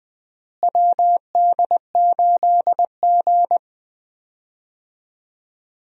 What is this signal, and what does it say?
Morse code, heard as a keyed tone, about 21 wpm, 700 Hz. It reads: WD8G
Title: USA Call Signs - 4 Characters - Repeated 2x - 20wpm